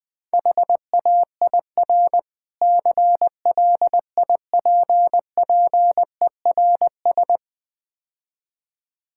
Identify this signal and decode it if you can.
Morse code, heard as a keyed tone, about 20 wpm, 700 Hz. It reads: HAIR CLIPPERS
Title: Difficult Words - Speed-Racing with Hints - 20wpm